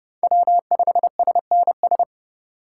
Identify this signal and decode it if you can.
Morse code, heard as a keyed tone, about 30 wpm, 700 Hz. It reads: W5SNS
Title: USA Call Signs - 5 Characters - 3x Character Spacing - 30wpm